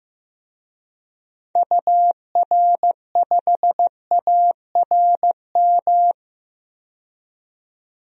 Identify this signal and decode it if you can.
Morse code, heard as a keyed tone, about 15 wpm, 700 Hz. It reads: UR5ARM
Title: DXCC-100 Call Signs - 6 Characters - 15wpm